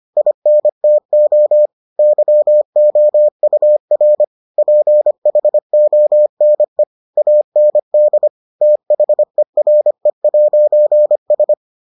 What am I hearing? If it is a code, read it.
Morse code, heard as a keyed tone, about 25 wpm, 600 Hz. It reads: INTO YOUR PHONE AND THERE'S